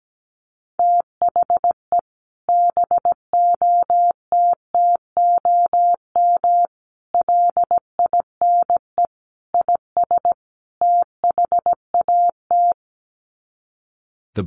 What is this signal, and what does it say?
Morse code, heard as a keyed tone, about 17 wpm, 700 Hz. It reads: THE BOTTOM LINE IS THAT
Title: Sets of 5 Words Encore 17wpm